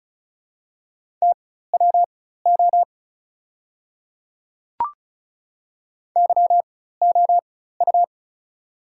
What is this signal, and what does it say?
Morse code, heard as a keyed tone, about 36 wpm, spaced as for 16 wpm, 700 Hz. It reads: TWO YOU
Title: 4x Character Spacing - Top 100 Words - 35wpm